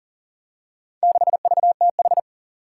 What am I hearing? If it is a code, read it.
Morse code, heard as a keyed tone, about 40 wpm, 700 Hz. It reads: 6VTH
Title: Four Challenging Letter-Number Combinations 40wpm